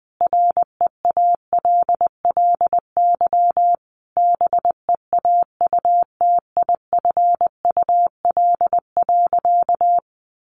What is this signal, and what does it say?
Morse code, heard as a keyed tone, about 20 wpm, 700 Hz. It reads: REALLY BEAUTIFUL.
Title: Sentences from Top 1100 Words 20wpm